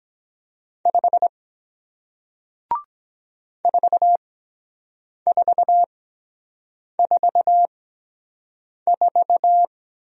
Morse code, heard as a keyed tone, about 23 wpm, spaced as for 7 wpm, 700 Hz. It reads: 5 4444